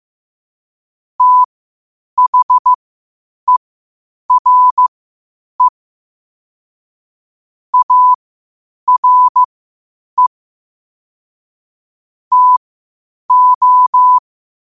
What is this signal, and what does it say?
Morse code, heard as a keyed tone, about 15 wpm, spaced as for 8 wpm, 1000 Hz. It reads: THERE ARE TO
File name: qotdfiles / 8wpm